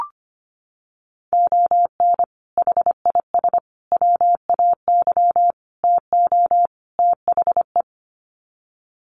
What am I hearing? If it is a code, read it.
Morse code, heard as a keyed tone, about 25 wpm, 700 Hz. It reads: ON HIS WAY TO THE
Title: Sets of 5 Words Encore2 25wpm